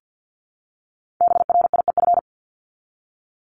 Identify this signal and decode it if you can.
Morse code, heard as a keyed tone, about 50 wpm, 700 Hz. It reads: BRIEF
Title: LY Suffix - Top 100 Words with Base Words 50wpm